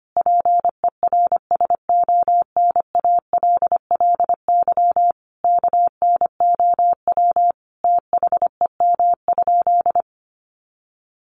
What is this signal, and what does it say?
Morse code, heard as a keyed tone, about 25 wpm, 700 Hz. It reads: PERSONALLY KNOW THEM?
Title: Sentences from Top 1400 Words 25wpm